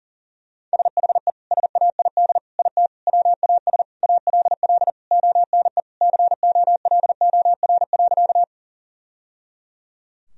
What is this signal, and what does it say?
Morse code, heard as a keyed tone, about 40 wpm, 700 Hz. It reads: SHE SAID IT WAS ALL ONE COLOR.
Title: Sentences from Top 1000 Words - Review Part 2 - 40wpm